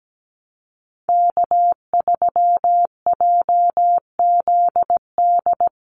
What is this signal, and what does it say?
Morse code, heard as a keyed tone, about 17 wpm, 700 Hz. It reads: K3JZD
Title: NA SOTA Chaser Callsigns 17wpm